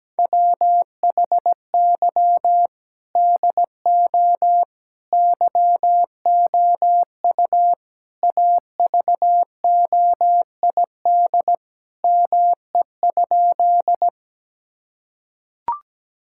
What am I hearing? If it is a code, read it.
Morse code, heard as a keyed tone, about 17 wpm, 700 Hz. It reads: WHY DO YOU AVOID ME?
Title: Sentences from Top 1000 Words - Review Part 6 - 17wpm